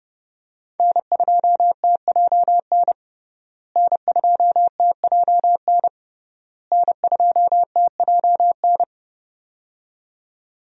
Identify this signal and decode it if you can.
Morse code, heard as a keyed tone, about 30 wpm, 700 Hz. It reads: N2TJN N2TJN N2TJN
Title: USA Call Signs - 5 Characters - Repeated 3x - 30wpm